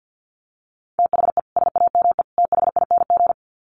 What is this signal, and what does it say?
Morse code, heard as a keyed tone, about 50 wpm, 700 Hz. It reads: THE SAME THING
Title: Sets of 3 Words 50wpm